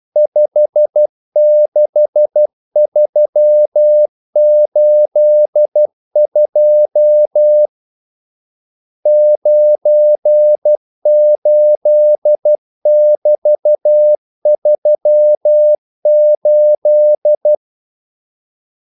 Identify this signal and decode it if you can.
Morse code, heard as a keyed tone, about 12 wpm, 600 Hz. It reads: 56382 98<BT>38
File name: code-exemple